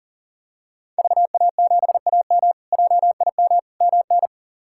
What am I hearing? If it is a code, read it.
Morse code, heard as a keyed tone, about 40 wpm, 700 Hz. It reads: VA7AM JIM MN
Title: K1USN SST QSO Practice - 40wpm